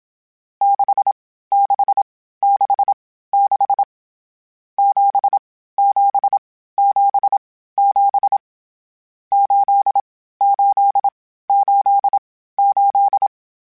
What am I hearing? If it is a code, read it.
Morse code, heard as a keyed tone, about 27 wpm, spaced as for 15 wpm, 800 Hz. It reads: 6666 7777 8888